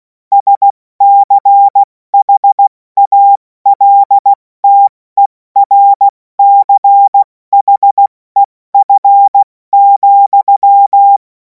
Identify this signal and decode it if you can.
Morse code, heard as a keyed tone, about 16 wpm, 800 Hz. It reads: SCHALTERCHEF,
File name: Morsekurs / Witze / Witz 31. März 2026 70 F80